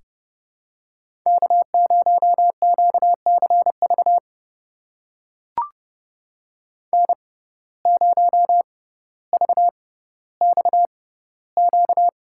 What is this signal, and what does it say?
Morse code, heard as a keyed tone, about 30 wpm, spaced as for 15 wpm, 700 Hz. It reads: K0QCV N0VXQ
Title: USA Call Signs - 5 Characters - 6x Character Spacing - 30wpm